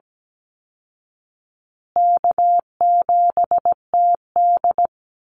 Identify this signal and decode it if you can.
Morse code, heard as a keyed tone, about 17 wpm, 700 Hz. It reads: K7TD
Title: Easy Call Signs Repeated 3x - 17wpm